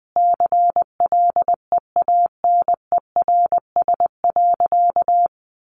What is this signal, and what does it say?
Morse code, heard as a keyed tone, about 20 wpm, 700 Hz. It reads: CLEANERS.
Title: Additional Easy Phrases 20wpm